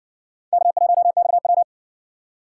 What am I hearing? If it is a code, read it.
Morse code, heard as a keyed tone, about 58 wpm, 700 Hz. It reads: K1XW